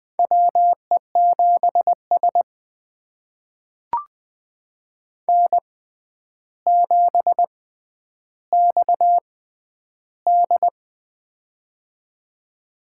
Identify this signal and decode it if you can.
Morse code, heard as a keyed tone, about 21 wpm, spaced as for 10 wpm, 700 Hz. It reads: WE7S N7XD
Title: USA Call Signs - 4 Characters - 6x Character Spacing - 20wpm